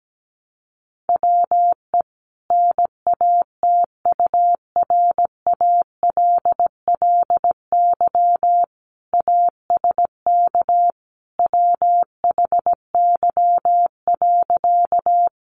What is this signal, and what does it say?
Morse code, heard as a keyed tone, about 17 wpm, 700 Hz. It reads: WE NATURALLY ASK WHY.
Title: Sentences from Top 1300 Words 17wpm